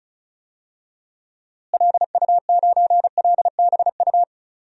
Morse code, heard as a keyed tone, about 36 wpm, 700 Hz. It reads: LU9LBU
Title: DXCC-100 Call Signs - 6 Characters - Repeated 2x - 35wpm